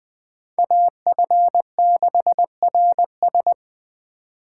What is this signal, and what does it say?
Morse code, heard as a keyed tone, about 24 wpm, 700 Hz. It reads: AF6RS
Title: Speed Racing - USA 2x2 Call Signs 20wpm